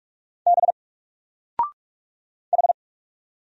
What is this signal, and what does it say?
Morse code, heard as a keyed tone, about 44 wpm, 700 Hz. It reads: B H
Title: Single Letter-Number 45wpm